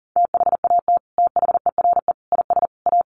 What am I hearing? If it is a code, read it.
Morse code, heard as a keyed tone, about 40 wpm, 700 Hz. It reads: THAT THERE IS A
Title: Sets of 4 Words 40wpm